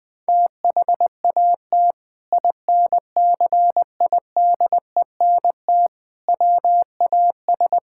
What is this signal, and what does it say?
Morse code, heard as a keyed tone, about 20 wpm, 700 Hz. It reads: THAT INCIDENT WAS
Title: Sentences with 8-Word Phrases - Spoken Beginning - 20wpm